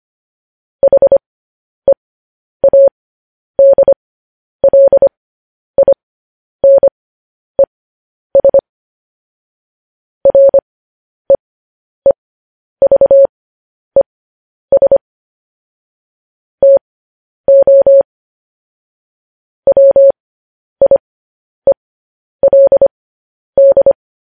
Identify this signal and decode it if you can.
Morse code, heard as a keyed tone, about 25 wpm, spaced as for 10 wpm, 550 Hz. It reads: HEADLINES REEVES TO WIELD
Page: news-headlines-in-morse-code-at-10-wpm